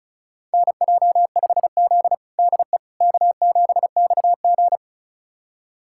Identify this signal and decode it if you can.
Morse code, heard as a keyed tone, about 36 wpm, 700 Hz. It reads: NJ5Z DE K7XG
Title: USA Callsign DE Callsign - 4 Characters - 35wpm